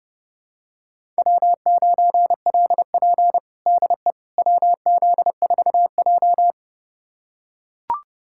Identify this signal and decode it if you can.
Morse code, heard as a keyed tone, about 30 wpm, 700 Hz. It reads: W9LP DE WZ4J
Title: USA Callsign DE Callsign - 4 Characters - 30wpm